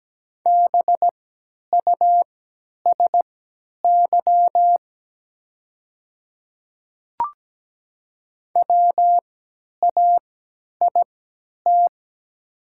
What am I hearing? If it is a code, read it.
Morse code, heard as a keyed tone, about 17 wpm, spaced as for 10 wpm, 700 Hz. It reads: BUSY WAIT
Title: 3x Character Spacing - Top 500 Words - 17wpm